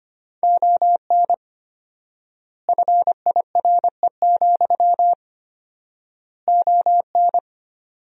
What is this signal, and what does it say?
Morse code, heard as a keyed tone, about 25 wpm, 700 Hz. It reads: ON FIRE, ON
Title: 4x Word-Spacing - Sentences from Top 400 Words - 25wpm